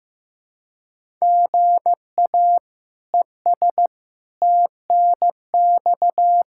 Morse code, heard as a keyed tone, about 15 wpm, 700 Hz. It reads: GA ES TNX
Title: K1USN SST QSO Practice 15wpm